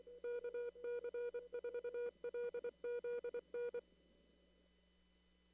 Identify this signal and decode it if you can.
Morse code, heard as a keyed tone, about 24 wpm, 480 Hz. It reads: KC4LZN
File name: CW / kt5j wpx cw-op-k5pi